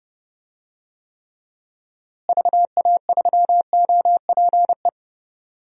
Morse code, heard as a keyed tone, about 31 wpm, 700 Hz. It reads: VA3OPE
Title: DXCC-100 Call Signs - 6 Characters - Repeated 2x - 30wpm